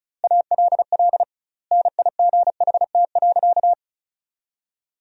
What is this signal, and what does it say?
Morse code, heard as a keyed tone, about 34 wpm, 700 Hz. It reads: ALL NIGHT.
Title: 2x Word-Spacing - Sentences from Top 300 Words - 35wpm